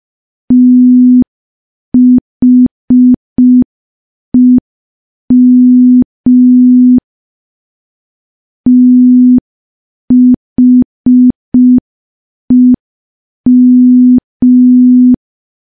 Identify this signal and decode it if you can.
Morse code, heard as a keyed tone, about 5 wpm, 250 Hz. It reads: THEM THEM
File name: morse / morse-training-05WPM